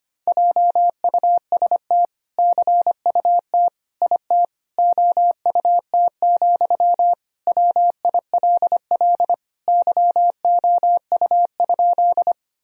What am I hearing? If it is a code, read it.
Morse code, heard as a keyed tone, about 25 wpm, 700 Hz. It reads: JUST CUT IT OUT, WILL YOU?
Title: Sentences from Top 1000 Words - Review Part 1 - 25wpm